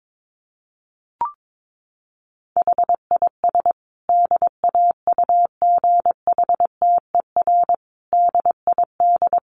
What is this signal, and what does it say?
Morse code, heard as a keyed tone, about 22 wpm, 700 Hz. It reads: HIS DAUGHTER DID
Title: Sentences from Top 1500 Words 22wpm